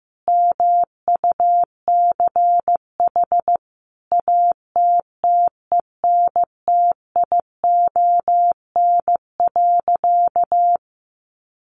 Morse code, heard as a keyed tone, about 15 wpm, 700 Hz. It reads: MUCH ATTENTION.
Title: Sentences from Top 1000 Words 15wpm